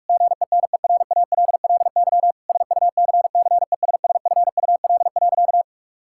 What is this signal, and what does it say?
Morse code, heard as a keyed tone, about 46 wpm, 700 Hz. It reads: GENERALLY SUCCESSFUL.
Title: Sentences from Top 1000 Words - Review Part 6 - 45wpm